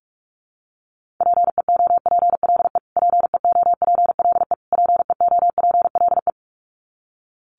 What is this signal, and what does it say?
Morse code, heard as a keyed tone, about 44 wpm, 700 Hz. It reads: PEOPLE PEOPLE PEOPLE
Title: Top 500 Words Repeated 45wpm